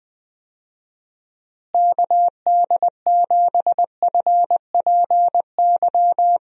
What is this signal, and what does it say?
Morse code, heard as a keyed tone, about 20 wpm, 700 Hz. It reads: KD7FPY
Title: USA Call Signs - 6 Characters - Repeated 3x - 20wpm